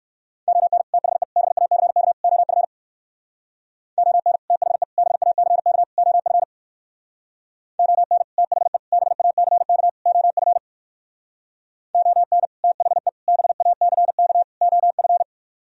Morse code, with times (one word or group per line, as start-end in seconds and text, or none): 0.5-0.8 ON
0.9-1.2 THE
1.4-2.1 BACK
2.2-2.6 OF
4.0-4.4 ON
4.5-4.8 THE
5.0-5.8 BACK
6.0-6.4 OF
7.8-8.2 ON
8.4-8.8 THE
8.9-9.9 BACK
10.1-10.6 OF
11.9-12.5 ON
12.6-13.1 THE
13.3-14.4 BACK
14.6-15.2 OF